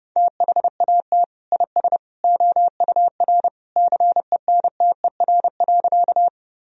Morse code, heard as a keyed tone, about 30 wpm, 700 Hz. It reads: THAT IS OUR CENTER.
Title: Sentences from Top 1000 Words - Review Part 2 - 30wpm